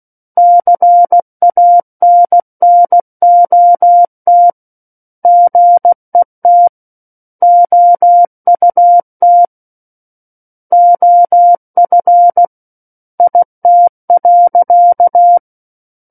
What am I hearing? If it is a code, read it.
Morse code, heard as a keyed tone, about 16 wpm, 700 Hz. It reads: CANNOT GET OUT OF IT.
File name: Book One / 07-Chapter 7 - 16 WPM